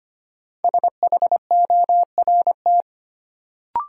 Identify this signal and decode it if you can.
Morse code, heard as a keyed tone, about 25 wpm, 700 Hz. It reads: SHORT
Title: Sets of 5 Words Encore - New Word Drill - 25wpm